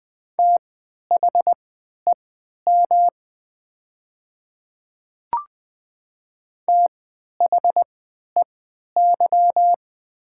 Morse code, heard as a keyed tone, about 20 wpm, spaced as for 11 wpm, 700 Hz. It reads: THEM THEY
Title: 3x Character Spacing - Top 100 Words - 20wpm